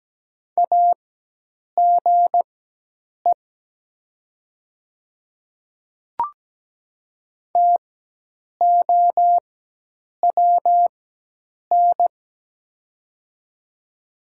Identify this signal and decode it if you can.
Morse code, heard as a keyed tone, about 17 wpm, spaced as for 8 wpm, 700 Hz. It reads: AGE TOWN
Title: 4x Character Spacing - Top 500 Words - 17wpm